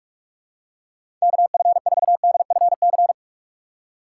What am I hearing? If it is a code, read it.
Morse code, heard as a keyed tone, about 41 wpm, 700 Hz. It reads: KF4DFC
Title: USA Call Signs - 6 Characters - 2x Character Spacing - 45wpm